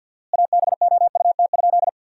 Morse code, heard as a keyed tone, about 50 wpm, 700 Hz. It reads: ABOUT?
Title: 4x Word-Spacing - Sentences from Top 400 Words - 50wpm